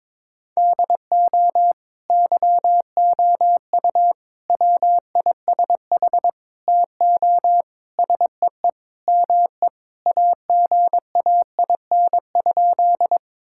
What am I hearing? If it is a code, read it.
Morse code, heard as a keyed tone, about 22 wpm, 700 Hz. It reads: DO YOU WISH TO SEE ME AGAIN?
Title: Sentences from Top 1600 Words 22wpm